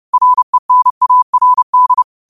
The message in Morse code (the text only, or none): RENARD